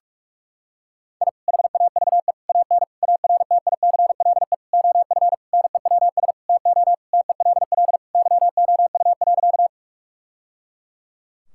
Morse code, heard as a keyed tone, about 42 wpm, 700 Hz. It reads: I HAVE AN ARTICLE OF NEWS TO TELL YOU.
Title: Sentences from Top 1000 Words - Review Part 5 - 45wpm